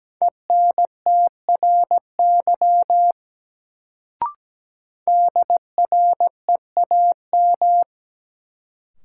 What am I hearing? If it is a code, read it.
Morse code, heard as a keyed tone, about 17 wpm, 700 Hz. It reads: ENTRY DREAM
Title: Sentences from Top 1500 Words - New Word Drill - 17wpm